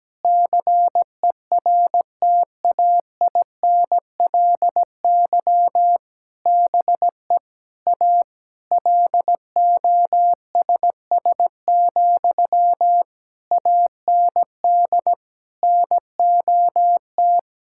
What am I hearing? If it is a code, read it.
Morse code, heard as a keyed tone, about 17 wpm, 700 Hz. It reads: CERTAINLY BE A LOSS, AND NOT
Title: Sentences from Top 1600 Words 17wpm